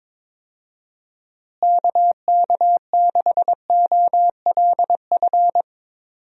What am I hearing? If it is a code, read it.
Morse code, heard as a keyed tone, about 22 wpm, 700 Hz. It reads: KK6OLF
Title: USA Call Signs - 6 Characters - Repeated 3x - 22wpm